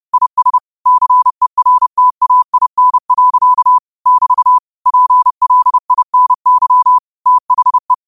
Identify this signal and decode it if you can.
Morse code, heard as a keyed tone, about 30 wpm, 1000 Hz. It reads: IS CERTAIN. <BT> PLINY THE